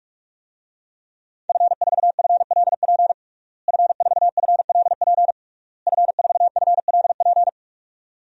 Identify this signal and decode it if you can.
Morse code, heard as a keyed tone, about 44 wpm, 700 Hz. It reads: F4FLP F4FLP F4FLP
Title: DXCC-100 Call Signs - 5 Characters - Repeated 3x - 45wpm